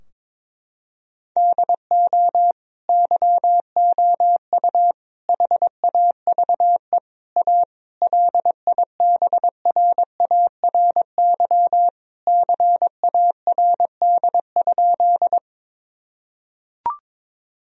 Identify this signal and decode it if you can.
Morse code, heard as a keyed tone, about 21 wpm, 700 Hz. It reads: DO YOU HAVE A LIBRARY CARD?
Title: Sentences from Top 1000 Words - Review Part 4 - 22wpm